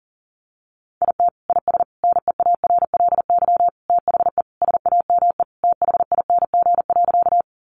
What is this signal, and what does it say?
Morse code, heard as a keyed tone, about 40 wpm, 700 Hz. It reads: IT IS NEARLY THE SAME THING.
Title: Sentences from Top 1300 Words 40wpm